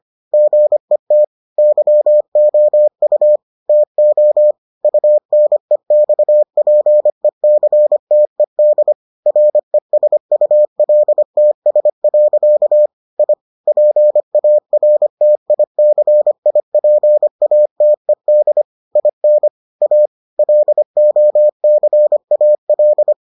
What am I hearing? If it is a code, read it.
Morse code, heard as a keyed tone, about 25 wpm, 600 Hz. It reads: GET YOU TO UNEXPECTED RESULTS. I PARTICIPATED IN A LOCAL